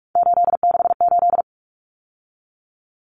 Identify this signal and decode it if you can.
Morse code, heard as a keyed tone, about 44 wpm, 700 Hz. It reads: 868
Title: Numbers - 3-Digit 45wpm